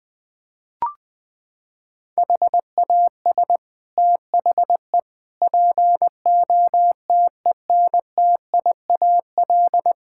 Morse code, heard as a keyed tone, about 20 wpm, 700 Hz. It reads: HAS THE POTENTIAL
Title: Sets of 6 Words - 20wpm